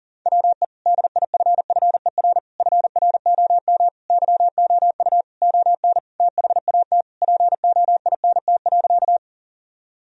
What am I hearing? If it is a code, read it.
Morse code, heard as a keyed tone, about 40 wpm, 700 Hz. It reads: WE DIFFER FROM YOU ON THAT POINT.
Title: Sentences from Top 1000 Words - Review Part 1 - 40wpm